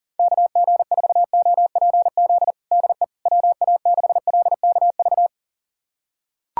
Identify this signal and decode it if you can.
Morse code, heard as a keyed tone, about 40 wpm, 700 Hz. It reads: KG4OPZ DE WA6LKV
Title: USA Callsign DE Callsign - 6 Characters - 40wpm